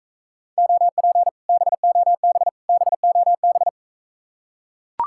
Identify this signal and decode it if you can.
Morse code, heard as a keyed tone, about 42 wpm, 700 Hz. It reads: OP BOB BOB